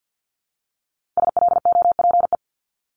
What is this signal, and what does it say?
Morse code, heard as a keyed tone, about 50 wpm, 700 Hz. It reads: SLOPE
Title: Multi-letter Phonemes 50wpm